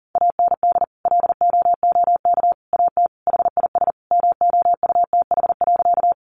Morse code, heard as a keyed tone, about 40 wpm, 700 Hz. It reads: AND LOOK AT HIS MOUTH.
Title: Sentences from Top 1500 Words 40wpm